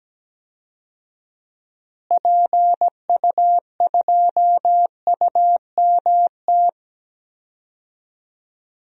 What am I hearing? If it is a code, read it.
Morse code, heard as a keyed tone, about 17 wpm, 700 Hz. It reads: PU2UMT
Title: DXCC-100 Call Signs - 6 Characters - Repeated 3x - 17wpm